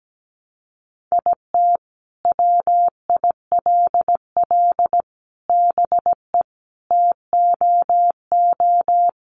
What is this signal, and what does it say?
Morse code, heard as a keyed tone, about 17 wpm, 700 Hz. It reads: IT WILL BE TOO
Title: Sentences from Top 300 Words 17wpm